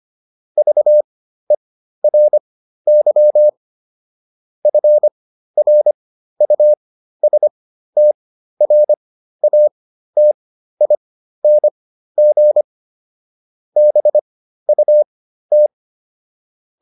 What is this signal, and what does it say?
Morse code, heard as a keyed tone, about 25 wpm, spaced as for 13 wpm, 600 Hz. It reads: VERY FRUSTRATING BUT